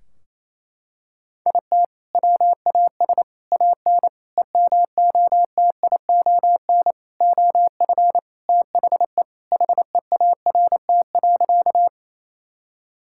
Morse code, heard as a keyed tone, about 28 wpm, 700 Hz. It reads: IT WAS AN EMOTION OF THE HEART.
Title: Sentences from Top 1600 Words 28wpm